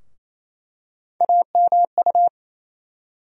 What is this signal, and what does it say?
Morse code, heard as a keyed tone, about 28 wpm, 700 Hz. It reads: AMU
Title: USA Call Sign Postfixes - 3 Letters 28wpm